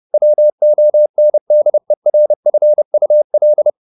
Morse code, heard as a keyed tone, about 30 wpm, 600 Hz. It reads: WONDERFUL
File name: the old man and the sea / the old man and the sea 1 30wpm